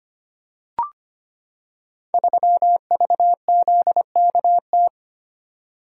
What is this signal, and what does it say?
Morse code, heard as a keyed tone, about 25 wpm, 700 Hz. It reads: 3VZKT
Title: Code Groups - 5 Letter:Numbers - 25wpm